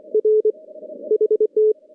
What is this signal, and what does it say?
Morse code, heard as a keyed tone, about 24 wpm, 430 Hz. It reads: R HT
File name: contests / CW-SS-k5tr- -2200